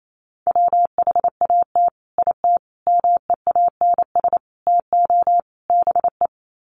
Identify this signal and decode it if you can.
Morse code, heard as a keyed tone, about 28 wpm, 700 Hz. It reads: WHAT IT MEANS TO BE
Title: Sets of 5 Words 28wpm